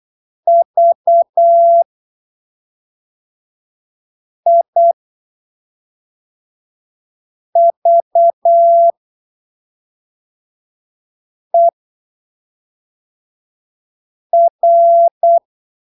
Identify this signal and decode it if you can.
Morse code, heard as a keyed tone, about 8 wpm, spaced as for 3 wpm, 680 Hz. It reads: VIVER
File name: arquivos / texto / texto 21 08ppm